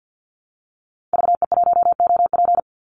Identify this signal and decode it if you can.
Morse code, heard as a keyed tone, about 48 wpm, 700 Hz. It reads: VE1OP
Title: Call Signs Repeated 3x - 50wpm